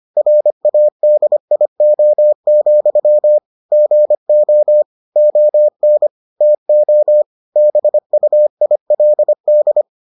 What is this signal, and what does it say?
Morse code, heard as a keyed tone, about 25 wpm, 600 Hz. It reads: RADIO, GO ON TO BUILD